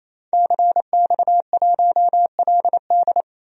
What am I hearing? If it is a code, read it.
Morse code, heard as a keyed tone, about 28 wpm, 700 Hz. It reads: CX1LD
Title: DXCC-100 Call Signs - 5 Characters - Repeated 3x - 28wpm